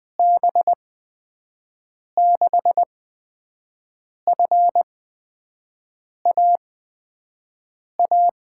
Morse code, heard as a keyed tone, about 20 wpm, spaced as for 5 wpm, 700 Hz. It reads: B6FAA
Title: USA Call Signs - 6 Characters - 8x Character Spacing - 20wpm